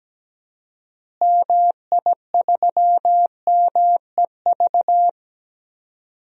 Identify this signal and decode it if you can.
Morse code, heard as a keyed tone, about 17 wpm, 700 Hz. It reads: MI3MEV
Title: DXCC-100 Call Signs - 6 Characters - Repeated 2x - 17wpm